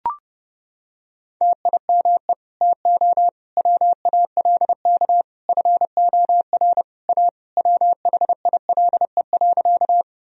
Morse code, heard as a keyed tone, about 30 wpm, 700 Hz. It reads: TIME TO WALK FOR A WHILE.
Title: Sentences from Top 1000 Words - Review Part 1 - 30wpm